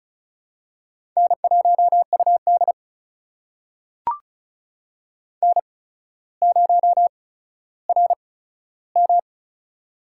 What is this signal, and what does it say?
Morse code, heard as a keyed tone, about 36 wpm, spaced as for 15 wpm, 700 Hz. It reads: N1UD N0RM
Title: USA Call Signs - 4 Characters - 8x Character Spacing - 35wpm